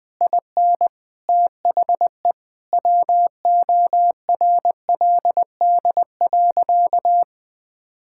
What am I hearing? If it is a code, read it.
Morse code, heard as a keyed tone, about 21 wpm, 700 Hz. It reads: IN THE WORLD.
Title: Sentences with 9-Word Phrases - 20wpm